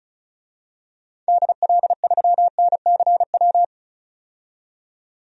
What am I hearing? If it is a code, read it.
Morse code, heard as a keyed tone, about 36 wpm, 700 Hz. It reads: DL3NCW
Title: DXCC-100 Call Signs - 6 Characters - 35wpm